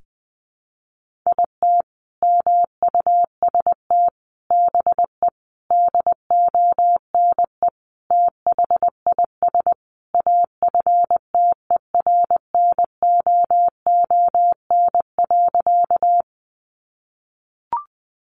Morse code, heard as a keyed tone, about 19 wpm, 700 Hz. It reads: IT MUST BE DONE THIS AFTERNOON.
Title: Sentences from Top 1500 Words 20wpm